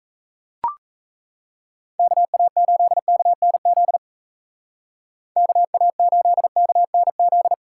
Morse code, heard as a keyed tone, about 37 wpm, 700 Hz. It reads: KA8KNZ KA8KNZ